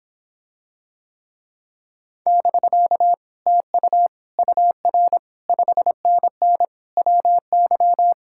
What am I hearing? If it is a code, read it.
Morse code, heard as a keyed tone, about 26 wpm, 700 Hz. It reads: <BK> TU UR 5NN WY